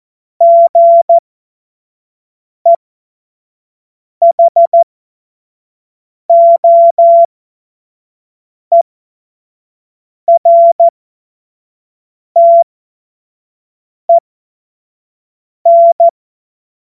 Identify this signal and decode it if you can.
Morse code, heard as a keyed tone, about 14 wpm, spaced as for 5 wpm, 680 Hz. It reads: GEHOERTEN
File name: Morsekurs / Witze / Witz 13. April 2021, 25 F70